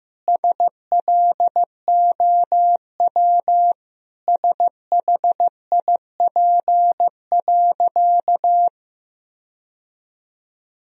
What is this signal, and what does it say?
Morse code, heard as a keyed tone, about 15 wpm, 700 Hz. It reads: SLOW SHIP.
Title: Sentences from Top 1000 Words - Review Part 2 - 15wpm